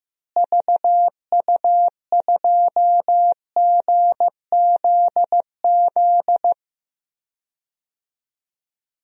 Morse code, heard as a keyed tone, about 15 wpm, 700 Hz. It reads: VU2GZZ
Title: DXCC-100 Call Signs - 6 Characters - Repeated 3x - 15wpm